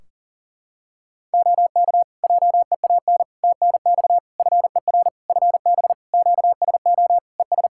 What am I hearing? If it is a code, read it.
Morse code, heard as a keyed tone, about 37 wpm, 700 Hz. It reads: OK JEAN TNX FER FB QSO ES
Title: Practice QSO - Exchange 4 - Ending - 40wpm